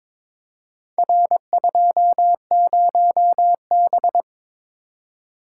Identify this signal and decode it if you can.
Morse code, heard as a keyed tone, about 22 wpm, 700 Hz. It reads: R20B
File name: practice / 22wpm / Four Letter-Number Combinations - 22wpm